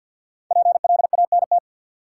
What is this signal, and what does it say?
Morse code, heard as a keyed tone, about 52 wpm, 700 Hz. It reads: PLANT
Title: Sentences from Top 1000 Words - Review Part 1 - New Word Drill - 50wpm